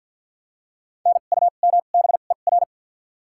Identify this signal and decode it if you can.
Morse code, heard as a keyed tone, about 52 wpm, 700 Hz. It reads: N U M B E R
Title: 2x Character Spacing - Top 200 Words - 50wpm